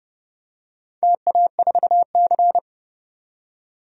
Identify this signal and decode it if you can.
Morse code, heard as a keyed tone, about 30 wpm, 700 Hz. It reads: TA4C
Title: DXCC-100 Call Signs - 4 Characters - 30wpm